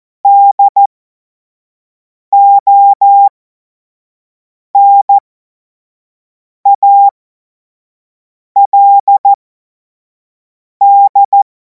Morse code, heard as a keyed tone, about 14 wpm, spaced as for 5 wpm, 800 Hz. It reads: DONALD
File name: Morsekurs / Witze / Witz 02. Juni 2026 25 F70